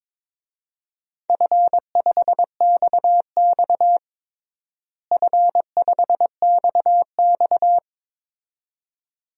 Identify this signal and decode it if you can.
Morse code, heard as a keyed tone, about 22 wpm, 700 Hz. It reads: F5XX F5XX
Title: DXCC-100 Call Signs - 4 Characters - Repeated 2x - 22wpm